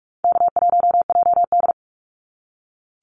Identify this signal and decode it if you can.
Morse code, heard as a keyed tone, about 44 wpm, 700 Hz. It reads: K1JD
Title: NA SOTA Activator Callsigns 45wpm